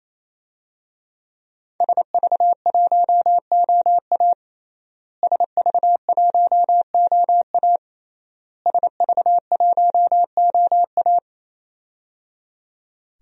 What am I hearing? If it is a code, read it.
Morse code, heard as a keyed tone, about 27 wpm, 700 Hz. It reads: SV1OA SV1OA SV1OA
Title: DXCC-100 Call Signs - 5 Characters - Repeated 3x - 28wpm